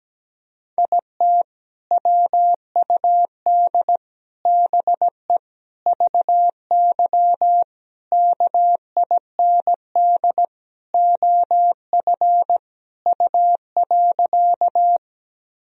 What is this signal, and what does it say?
Morse code, heard as a keyed tone, about 17 wpm, 700 Hz. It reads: IT WUD BE VY KIND OF U.
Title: Sentences from Top 200 Words and Abbreviations 17wpm